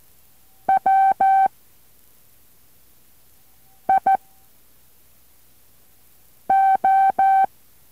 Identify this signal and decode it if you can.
Morse code, heard as a keyed tone, about 14 wpm, 2350 Hz, with noise. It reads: W I O